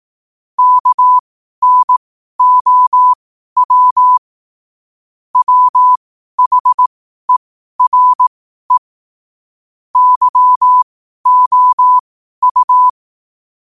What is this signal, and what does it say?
Morse code, heard as a keyed tone, about 18 wpm, spaced as for 12 wpm, 1000 Hz. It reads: KNOW WHERE YOU
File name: qotdfiles / 12wpm